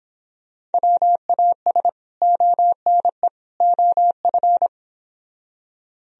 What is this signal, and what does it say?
Morse code, heard as a keyed tone, about 31 wpm, 700 Hz. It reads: WAS ONE OF